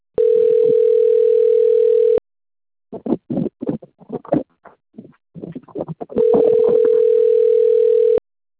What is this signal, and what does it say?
Ringback tone (North America), 440 + 480 Hz.
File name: out- - - - - .26363